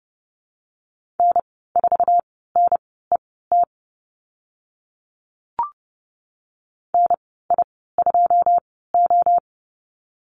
Morse code, heard as a keyed tone, about 30 wpm, spaced as for 17 wpm, 700 Hz. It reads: N4NET NI2O
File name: Lesson 15g - Number 5 - Call Signs 30wpm